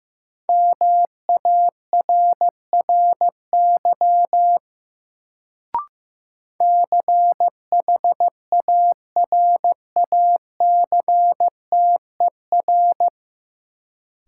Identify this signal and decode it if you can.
Morse code, heard as a keyed tone, about 15 wpm, 700 Hz. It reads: MARRY CHARACTER
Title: Sentences from Top 1200 Words - New Word Drill Alt - 15wpm